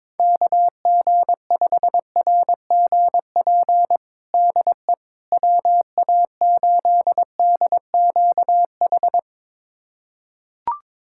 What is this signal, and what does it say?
Morse code, heard as a keyed tone, about 22 wpm, 700 Hz. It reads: KG5RGP DE WA8DQH
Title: USA Callsign DE Callsign 22wpm